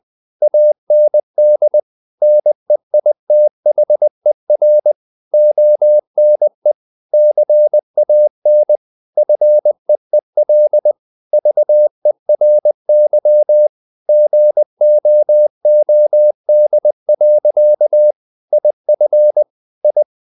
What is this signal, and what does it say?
Morse code, heard as a keyed tone, about 20 wpm, 600 Hz. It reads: AND NEITHER ONE CAN FEEL VERY GOOD. IF I
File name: the old man and the sea part 2 20wpm